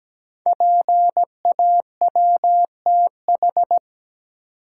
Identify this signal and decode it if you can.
Morse code, heard as a keyed tone, about 17 wpm, 700 Hz. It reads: PAWTH
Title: Code Groups - 5 Letters - 17wpm